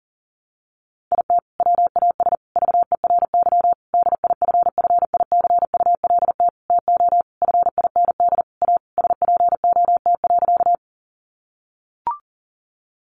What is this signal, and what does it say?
Morse code, heard as a keyed tone, about 40 wpm, 700 Hz. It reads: IT WAS VERY DIFFICULT TO FIND A SPOT.
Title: Sentences from Top 1100 Words 40wpm